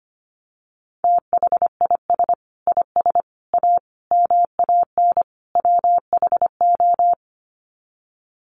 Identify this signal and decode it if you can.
Morse code, heard as a keyed tone, about 25 wpm, 700 Hz. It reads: THIS IS A MAN WHO
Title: Sets of 5 Words Encore2 25wpm